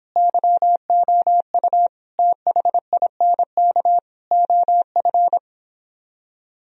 Morse code, heard as a keyed tone, about 31 wpm, 700 Hz. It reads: YOU THINK OF